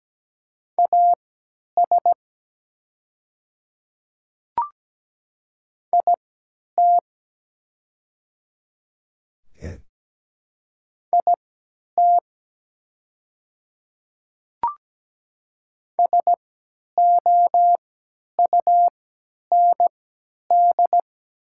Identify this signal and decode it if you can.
Morse code, heard as a keyed tone, about 17 wpm, spaced as for 10 wpm, 700 Hz. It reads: AS IT IT SOUND
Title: 3x Character Spacing - Top 200 Words - 17wpm